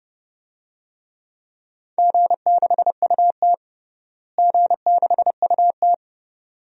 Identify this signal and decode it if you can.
Morse code, heard as a keyed tone, about 32 wpm, 700 Hz. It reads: G6UT G6UT
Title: DXCC-100 Call Signs - 4 Characters - Repeated 2x - 30wpm